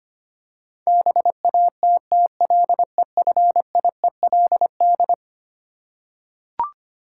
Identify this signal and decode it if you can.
Morse code, heard as a keyed tone, about 25 wpm, 700 Hz. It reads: BATTLEFIELD
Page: morse-code-ninja-practice-25wpm